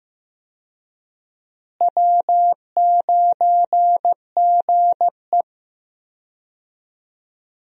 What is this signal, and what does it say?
Morse code, heard as a keyed tone, about 15 wpm, 700 Hz. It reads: W9GE
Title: USA Call Signs - 4 Characters - Repeated 2x - 15wpm